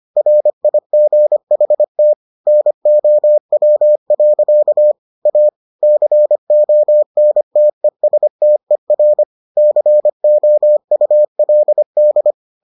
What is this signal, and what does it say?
Morse code, heard as a keyed tone, about 25 wpm, 600 Hz. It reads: RIGHT NOW. A CONTESTER COULD